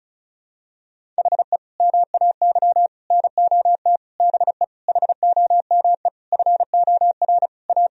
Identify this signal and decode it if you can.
Morse code, heard as a keyed tone, about 36 wpm, 700 Hz. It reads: HE MAY NOT BE HOME FOR A
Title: Sentences from Top 1000 Words - Review Part 2 - 35wpm